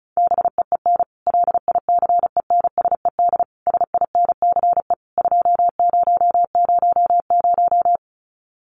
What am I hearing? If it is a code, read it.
Morse code, heard as a keyed tone, about 34 wpm, 700 Hz. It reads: BEEN LICENSED SINCE 2000
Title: Common QSO Phrases 35wpm